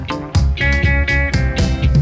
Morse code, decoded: none